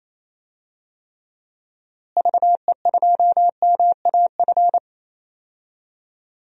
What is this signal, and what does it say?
Morse code, heard as a keyed tone, about 28 wpm, 700 Hz. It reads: VE2MAF
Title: DXCC-100 Call Signs - 6 Characters - Repeated 3x - 28wpm